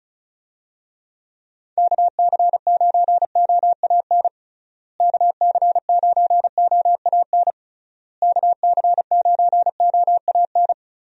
Morse code, heard as a keyed tone, about 36 wpm, 700 Hz. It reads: KC9OAN KC9OAN KC9OAN
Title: USA Call Signs - 6 Characters - Repeated 3x - 35wpm